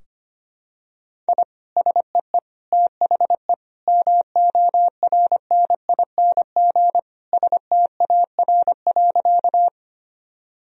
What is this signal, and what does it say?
Morse code, heard as a keyed tone, about 25 wpm, 700 Hz. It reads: I SEE THE MORNING STAR.
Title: Sentences from Top 1000 Words - Review Part 2 - 25wpm